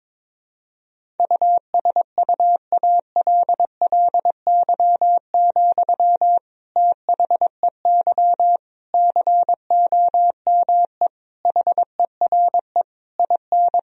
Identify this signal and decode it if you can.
Morse code, heard as a keyed tone, about 22 wpm, 700 Hz. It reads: USUALLY, THEY COME HERE IN
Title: Sentences from Top 1000 Words - Review Part 3 - 22wpm